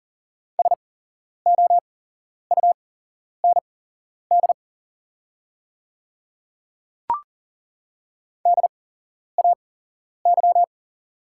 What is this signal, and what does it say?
Morse code, heard as a keyed tone, about 40 wpm, spaced as for 11 wpm, 690 Hz. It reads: SOUND DAY
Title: 8x Character Spacing - Top 100 Words - 40wpm